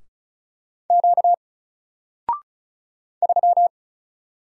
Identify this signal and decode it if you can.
Morse code, heard as a keyed tone, about 36 wpm, 700 Hz. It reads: Q 3
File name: Single-Letter-Number-35wpm